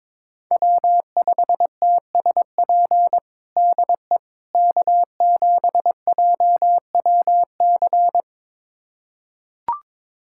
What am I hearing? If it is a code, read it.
Morse code, heard as a keyed tone, about 22 wpm, 700 Hz. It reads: W5TSP DE K7JWC
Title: USA Callsign DE Callsign - 5 Characters - 22wpm